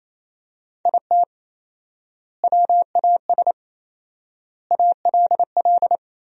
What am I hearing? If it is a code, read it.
Morse code, heard as a keyed tone, about 27 wpm, 700 Hz. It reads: IT WAS ALL
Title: 4x Word-Spacing - Sentences from Top 400 Words - 28wpm